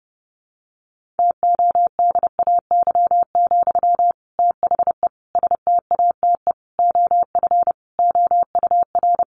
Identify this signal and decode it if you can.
Morse code, heard as a keyed tone, about 30 wpm, 700 Hz. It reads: TODAY, THE STATE OF OUR
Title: Sentences from Top 1400 Words 30wpm